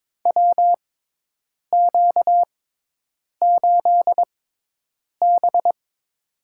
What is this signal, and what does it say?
Morse code, heard as a keyed tone, about 22 wpm, spaced as for 8 wpm, 700 Hz. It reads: WQ8B
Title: USA Call Signs - 4 Characters - 6x Character Spacing - 22wpm